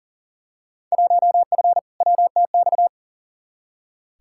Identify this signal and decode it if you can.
Morse code, heard as a keyed tone, about 40 wpm, 700 Hz. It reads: 1F WTX
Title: Field Day Exchanges - 40wpm